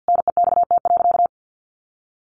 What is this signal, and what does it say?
Morse code, heard as a keyed tone, about 50 wpm, 700 Hz. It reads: NEXT.
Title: Sentences from Top 1500 Words 50wpm